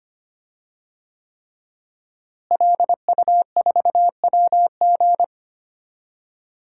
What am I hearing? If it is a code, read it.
Morse code, heard as a keyed tone, about 25 wpm, 700 Hz. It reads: LU4WG
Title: DXCC-100 Call Signs - 5 Characters - Repeated 3x - 25wpm